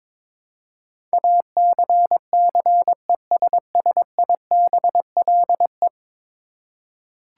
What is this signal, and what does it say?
Morse code, heard as a keyed tone, about 22 wpm, 700 Hz. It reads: ACCESSIBLE
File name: practice / 22wpm / 10 Letter Words - First 3 Letters Given - 22wpm